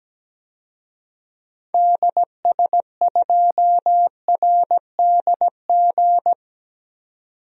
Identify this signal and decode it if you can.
Morse code, heard as a keyed tone, about 17 wpm, 700 Hz. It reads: DS2RDG
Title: DXCC-100 Call Signs - 6 Characters - 17wpm